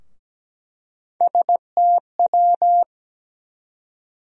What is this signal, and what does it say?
Morse code, heard as a keyed tone, about 17 wpm, 700 Hz. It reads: STW
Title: USA Call Sign Postfixes - 3 Letters 17wpm